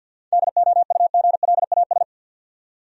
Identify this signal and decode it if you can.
Morse code, heard as a keyed tone, about 47 wpm, 700 Hz. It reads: DOUGLAS
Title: USA Common Names - 50wpm